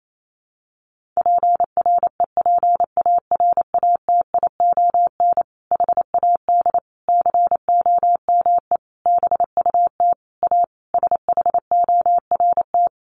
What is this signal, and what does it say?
Morse code, heard as a keyed tone, about 28 wpm, 700 Hz. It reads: PREPARATION HAD COME BUT A SHORT
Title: Sentences from Top 1400 Words 28wpm